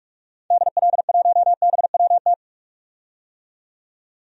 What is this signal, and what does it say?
Morse code, heard as a keyed tone, about 42 wpm, 700 Hz. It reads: DL1BWT
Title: DXCC-100 Call Signs - 6 Characters - 45wpm